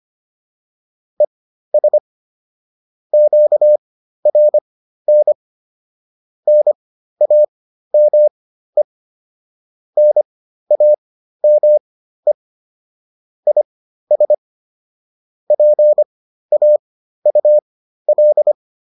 Morse code, heard as a keyed tone, about 25 wpm, spaced as for 13 wpm, 600 Hz. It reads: ES QRN NAME NAME IS PAUL
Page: intermediate-practice-files